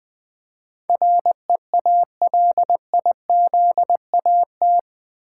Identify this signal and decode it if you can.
Morse code, heard as a keyed tone, about 20 wpm, 700 Hz. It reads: REALIZAT
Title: Build A Word - 4 Letters At A Time - 11-Letter Words - 20wpm